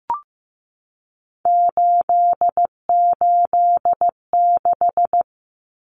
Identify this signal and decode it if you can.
Morse code, heard as a keyed tone, about 15 wpm, 700 Hz. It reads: 886
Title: Numbers - 3-Digit 15wpm